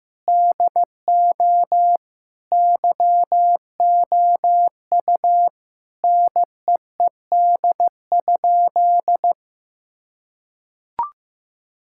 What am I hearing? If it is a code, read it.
Morse code, heard as a keyed tone, about 15 wpm, 700 Hz. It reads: DO YOU NEED?
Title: Sentences from Top 1000 Words - Review Part 5 - 15wpm